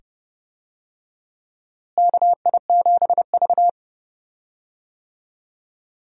Morse code, heard as a keyed tone, about 30 wpm, 700 Hz. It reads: KI7V
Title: USA Call Signs - 4 Characters - Repeated 2x - 30wpm